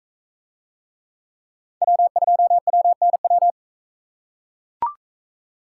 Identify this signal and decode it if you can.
Morse code, heard as a keyed tone, about 39 wpm, 700 Hz. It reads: W2WNW